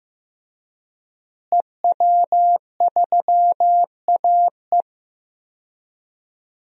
Morse code, heard as a keyed tone, about 15 wpm, 700 Hz. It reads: EW3AE
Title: DXCC-100 Call Signs - 5 Characters - Repeated 2x - 15wpm